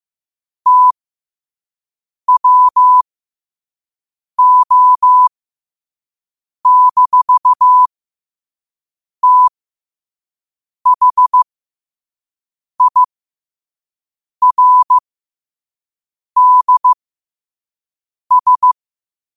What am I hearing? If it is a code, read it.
Morse code, heard as a keyed tone, about 15 wpm, spaced as for 5 wpm, 1000 Hz. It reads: TWO-THIRDS
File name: qotdfiles / 5wpm